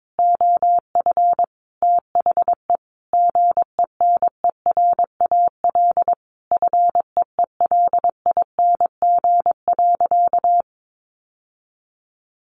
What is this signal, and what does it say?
Morse code, heard as a keyed tone, about 22 wpm, 700 Hz. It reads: OF THE GENERAL FEELING.
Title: Sentences from Top 1400 Words 22wpm